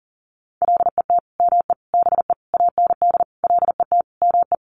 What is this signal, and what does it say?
Morse code, heard as a keyed tone, about 40 wpm, 700 Hz. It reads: LET ME BE AND LET ME
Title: Sentences from Top 300 Words 40wpm